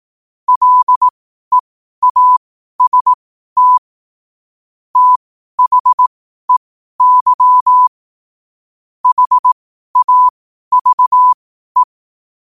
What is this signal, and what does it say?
Morse code, heard as a keyed tone, about 18 wpm, spaced as for 12 wpm, 1000 Hz. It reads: LEAST THEY HAVE
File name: qotdfiles / 12wpm